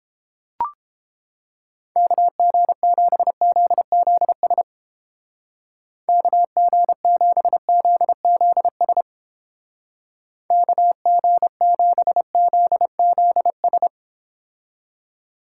Morse code, heard as a keyed tone, about 29 wpm, 700 Hz. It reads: KG7ZZS KG7ZZS KG7ZZS